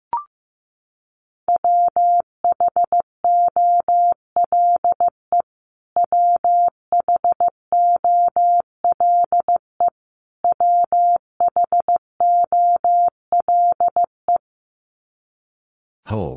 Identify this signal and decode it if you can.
Morse code, heard as a keyed tone, about 15 wpm, 700 Hz. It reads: WHOLE WHOLE WHOLE
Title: Top 500 Words Repeated 15wpm